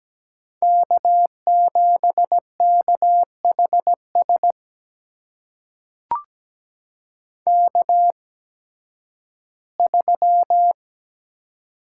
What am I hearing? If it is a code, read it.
Morse code, heard as a keyed tone, about 17 wpm, spaced as for 11 wpm, 700 Hz. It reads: K7KHS K3
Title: USA Call Signs - 5 Characters - 8x Character Spacing - 17wpm